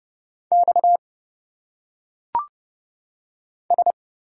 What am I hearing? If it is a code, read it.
Morse code, heard as a keyed tone, about 29 wpm, 700 Hz. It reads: X S
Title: Single Letters - Phonetic Pronunciation 30wpm